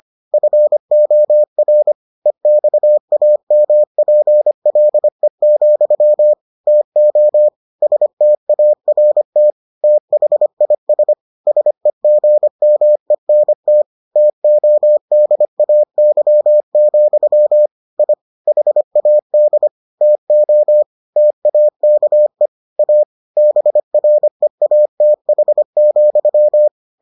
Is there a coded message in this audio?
Yes: Morse code from 0.3 to 26.7 s, heard as a keyed tone, about 25 wpm, 600 Hz. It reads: FOR EXAMPLE, TO START THIS SEGMENT TODAY, I HAD TO TAKE A BREATH,